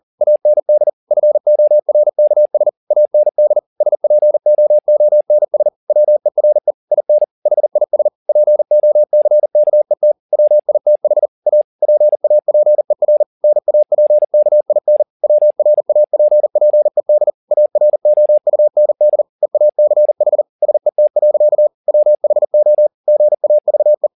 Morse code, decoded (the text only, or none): AND FORKS AND SPOONS WERE IN HIS POCKET WITH A PAPER NAPKIN WRAPPED AROUND EACH SET. WHO GAVE